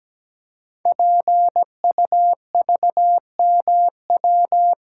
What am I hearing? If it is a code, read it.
Morse code, heard as a keyed tone, about 17 wpm, 700 Hz. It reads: PUVMW
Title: Code Groups - 5 Letters - 17wpm